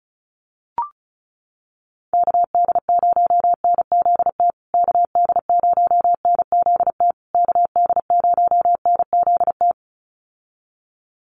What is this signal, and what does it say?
Morse code, heard as a keyed tone, about 34 wpm, 700 Hz. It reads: KD0NZT KD0NZT KD0NZT
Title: USA Call Signs Repeated 3x 35wpm